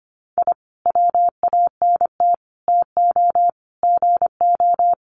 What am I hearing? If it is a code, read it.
Morse code, heard as a keyed tone, about 25 wpm, 700 Hz. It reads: I WANT TO GO
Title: Sets of 5 Words Encore 25wpm